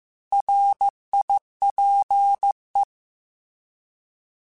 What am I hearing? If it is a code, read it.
Morse code, heard as a keyed tone, about 15 wpm, 800 Hz. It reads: RIPE